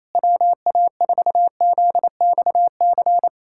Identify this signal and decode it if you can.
Morse code, heard as a keyed tone, about 28 wpm, 700 Hz. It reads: WA4ZXC
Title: USA Callsign DE Callsign 28wpm